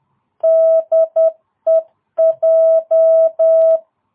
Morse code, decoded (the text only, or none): DEJ